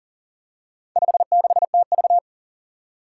Morse code, heard as a keyed tone, about 42 wpm, 700 Hz. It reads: 56TV
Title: Four Challenging Letter-Number Combinations 40wpm